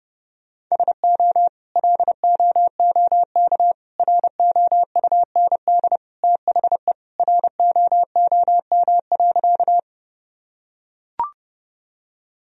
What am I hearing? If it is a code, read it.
Morse code, heard as a keyed tone, about 30 wpm, 700 Hz. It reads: SO LOOK ROUND THE ROOM.
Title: Sentences from Top 1000 Words - Review Part 1 - 30wpm